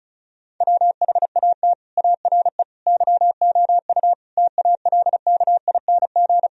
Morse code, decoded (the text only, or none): WHAT ARE YOU TALKING